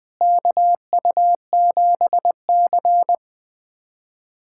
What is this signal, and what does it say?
Morse code, heard as a keyed tone, about 21 wpm, 700 Hz. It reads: KU7C
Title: USA Callsign DE Callsign - 4 Characters - 20wpm